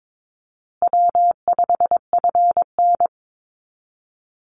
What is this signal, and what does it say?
Morse code, heard as a keyed tone, about 22 wpm, 700 Hz. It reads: W5FN
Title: Easy Call Signs Repeated 3x - 22wpm